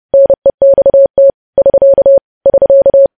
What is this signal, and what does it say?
Morse code, heard as a keyed tone, about 30 wpm, 550 Hz. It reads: NEXT <SK> <SK>
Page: news headlines in morse code at 30 wpm